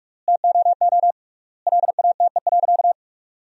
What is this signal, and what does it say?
Morse code, heard as a keyed tone, about 46 wpm, 700 Hz. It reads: TOO LATE.
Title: 3x Word-Spacing - Sentences from Top 300 Words - 45wpm